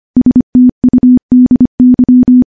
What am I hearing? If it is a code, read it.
Morse code, heard as a keyed tone, about 25 wpm, 250 Hz. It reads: STUDY
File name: morse / morse-training-25WPM